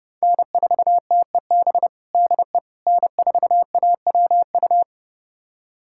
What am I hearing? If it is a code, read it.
Morse code, heard as a keyed tone, about 30 wpm, 700 Hz. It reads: N4TEB DE N4AWU
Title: USA Callsign DE Callsign - 5 Characters - 30wpm